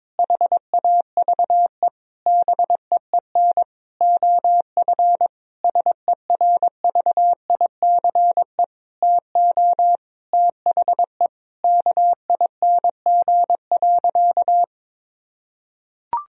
Morse code, heard as a keyed tone, about 22 wpm, 700 Hz. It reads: HAVE BEEN OF SERVICE TO THE KING.
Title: Sentences from Top 1000 Words - Review Part 4 - 22wpm